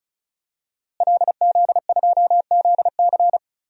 Morse code, heard as a keyed tone, about 36 wpm, 700 Hz. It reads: LZ2ZC
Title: DXCC-100 Call Signs - 5 Characters - Repeated 2x - 35wpm